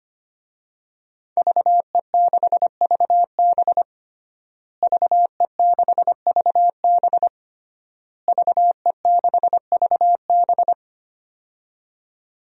Morse code, heard as a keyed tone, about 25 wpm, 700 Hz. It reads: VE6VB VE6VB VE6VB
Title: DXCC-100 Call Signs - 5 Characters - Repeated 3x - 25wpm